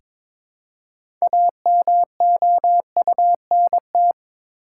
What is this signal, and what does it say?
Morse code, heard as a keyed tone, about 22 wpm, 700 Hz. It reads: AMOUNT
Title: Six Letter Words - 22wpm